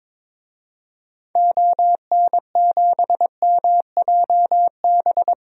Morse code, heard as a keyed tone, about 22 wpm, 700 Hz. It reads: ON7MJB
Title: DXCC-100 Call Signs - 6 Characters - Repeated 3x - 22wpm